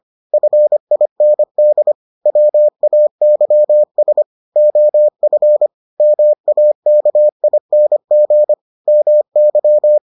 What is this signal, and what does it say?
Morse code, heard as a keyed tone, about 25 wpm, 600 Hz. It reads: FIND WAYS OF MAKING MY